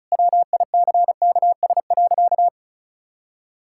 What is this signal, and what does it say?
Morse code, heard as a keyed tone, about 36 wpm, 700 Hz. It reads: WICKS.
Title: Pangram Challenge v2 - 35wpm